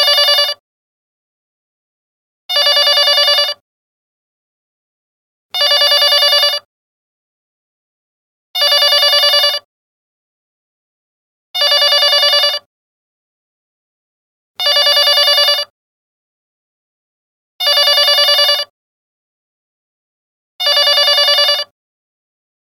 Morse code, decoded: none